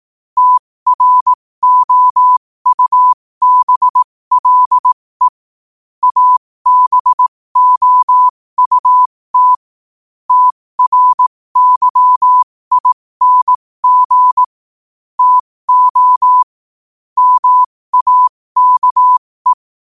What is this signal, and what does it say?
Morse code, heard as a keyed tone, about 18 wpm, spaced as for 15 wpm, 1000 Hz. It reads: TROUBLE ABOUT TRYING TO MAKE
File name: qotdfiles / 15wpm